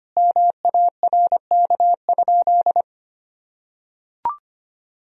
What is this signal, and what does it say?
Morse code, heard as a keyed tone, about 25 wpm, 700 Hz. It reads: MARK?
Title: Sentences from Top 1000 Words - Review Part 1 - 25wpm